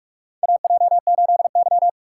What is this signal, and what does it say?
Morse code, heard as a keyed tone, about 46 wpm, 700 Hz. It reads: AJ8Y
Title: USA Call Signs - 4 Characters - 4x Character Spacing - 45wpm